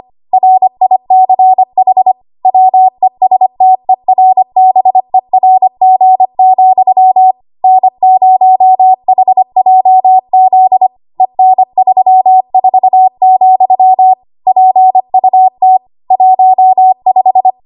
Morse code, heard as a keyed tone, about 25 wpm, 750 Hz. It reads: RICH WESTERBERG, N0HJZ EN34, PUT 15
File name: Morse / 25WPM